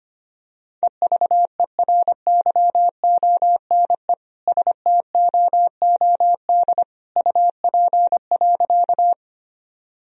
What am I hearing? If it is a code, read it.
Morse code, heard as a keyed tone, about 25 wpm, 700 Hz. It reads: EVERYONE STOOD UP.
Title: Sentences from Top 1000 Words - Review Part 2 - 25wpm